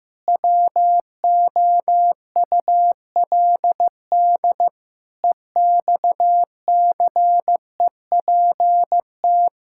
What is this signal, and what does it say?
Morse code, heard as a keyed tone, about 15 wpm, 700 Hz. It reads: WOULD EXCEPT
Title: Sentences from Top 1600 Words 15wpm